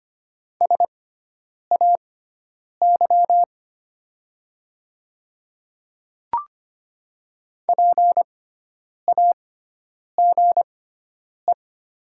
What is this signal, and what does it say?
Morse code, heard as a keyed tone, about 25 wpm, spaced as for 9 wpm, 700 Hz. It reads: SAY PAGE
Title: 6x Character Spacing - Top 500 Words - 25wpm